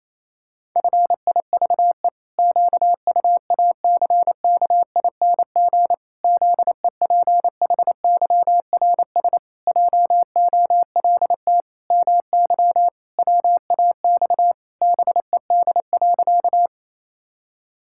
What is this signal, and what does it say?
Morse code, heard as a keyed tone, about 27 wpm, 700 Hz. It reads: FIVE QUACKING ZEPHYRS JOLT MY WAX BED.
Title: Pangram Challenge v2 - 28wpm